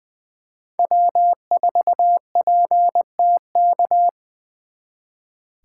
Morse code, heard as a keyed tone, about 25 wpm, 700 Hz. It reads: W4PTK
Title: Speed Racing - USA 1x3 Call Signs 20wpm